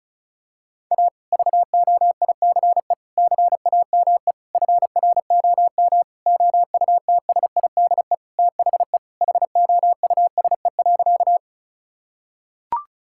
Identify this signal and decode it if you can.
Morse code, heard as a keyed tone, about 36 wpm, 700 Hz. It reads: A VOICE CAME FROM OUTSIDE THE HOUSE.
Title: Sentences from Top 1000 Words - Review Part 6 - 35wpm